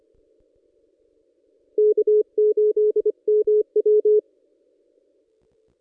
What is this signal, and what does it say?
Morse code, heard as a keyed tone, about 26 wpm, 420 Hz. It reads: K8MW
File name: contests / ss-cw-k5tr-op-n5rz -1600